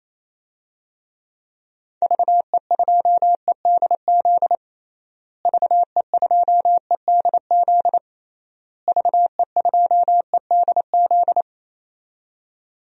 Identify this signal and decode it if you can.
Morse code, heard as a keyed tone, about 28 wpm, 700 Hz. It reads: VE2EDZ VE2EDZ VE2EDZ
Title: DXCC-100 Call Signs - 6 Characters - Repeated 3x - 28wpm